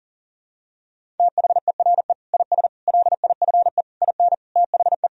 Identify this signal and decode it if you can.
Morse code, heard as a keyed tone, about 37 wpm, 700 Hz. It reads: THERE IS LIFE IN THE
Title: Sentences from Top 1000 Words - Review Part 1 - 40wpm